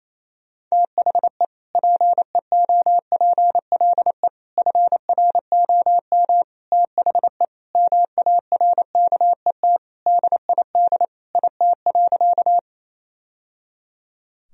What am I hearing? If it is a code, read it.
Morse code, heard as a keyed tone, about 28 wpm, 700 Hz. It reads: THE PEOPLE FROM THE MARKET DID IT.
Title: Sentences from Top 1000 Words - Review Part 3 - 28wpm